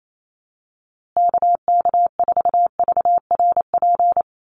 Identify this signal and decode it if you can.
Morse code, heard as a keyed tone, about 28 wpm, 700 Hz. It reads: KK4VRP
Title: USA Call Signs Repeated 3x 28wpm